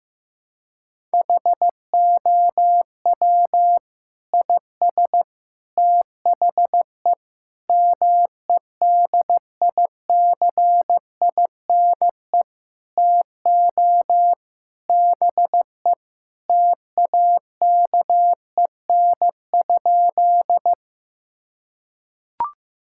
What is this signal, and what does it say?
Morse code, heard as a keyed tone, about 15 wpm, 700 Hz. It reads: HOW IS THE MEDICINE TO BE TAKEN?
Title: Sentences from Top 1000 Words - Review Part 6 - 15wpm